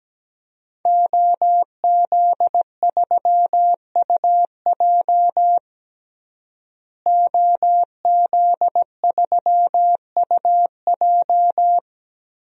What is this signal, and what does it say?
Morse code, heard as a keyed tone, about 17 wpm, 700 Hz. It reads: OZ3UJ OZ3UJ
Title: DXCC-100 Call Signs - 5 Characters - Repeated 2x - 17wpm